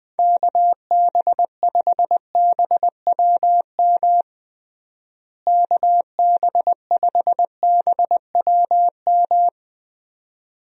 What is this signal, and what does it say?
Morse code, heard as a keyed tone, about 20 wpm, 700 Hz. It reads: KB5BWM KB5BWM
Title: USA Call Signs - 6 Characters - Repeated 2x - 20wpm